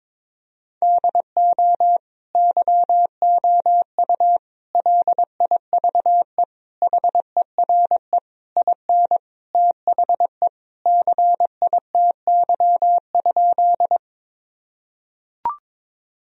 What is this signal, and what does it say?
Morse code, heard as a keyed tone, about 22 wpm, 700 Hz. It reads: DO YOU LIVE HERE IN THE CITY?
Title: Sentences from Top 1000 Words - Review Part 1 - 22wpm